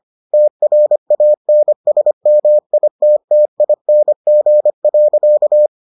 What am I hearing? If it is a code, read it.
Morse code, heard as a keyed tone, about 25 wpm, 600 Hz. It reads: TRANSMITTING.